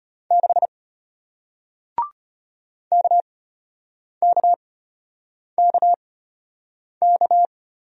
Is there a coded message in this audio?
Yes: Morse code from 0.3 to 7.5 s, heard as a keyed tone, about 34 wpm, spaced as for 8 wpm, 690 Hz. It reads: 6 KKKK